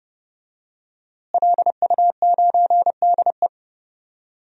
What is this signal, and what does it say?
Morse code, heard as a keyed tone, about 30 wpm, 700 Hz. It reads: LU9DE
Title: DXCC-100 Call Signs - 5 Characters - Repeated 3x - 30wpm